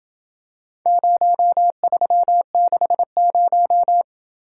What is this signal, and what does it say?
Morse code, heard as a keyed tone, about 27 wpm, 700 Hz. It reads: 0360